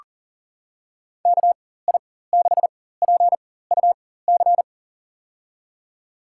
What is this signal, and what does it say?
Morse code, heard as a keyed tone, about 37 wpm, spaced as for 18 wpm, 700 Hz. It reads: KI6PUC
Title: USA Call Signs - 6 Characters - 4x Character Spacing - 40wpm